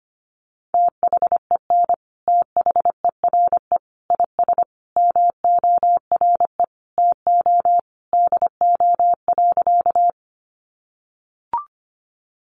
Morse code, heard as a keyed tone, about 25 wpm, 700 Hz. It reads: THEN THERE IS MORE TO DO.
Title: Sentences from Top 100 Words 25wpm